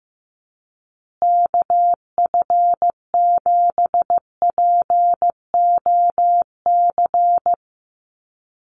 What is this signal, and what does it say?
Morse code, heard as a keyed tone, about 15 wpm, 700 Hz. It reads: KF7POC
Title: USA Call Signs Repeated 2x 15wpm